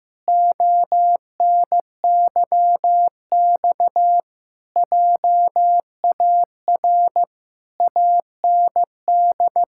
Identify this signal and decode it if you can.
Morse code, heard as a keyed tone, about 15 wpm, 700 Hz. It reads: ONYX JAR AND
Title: Pangram Challenge v2 - 15wpm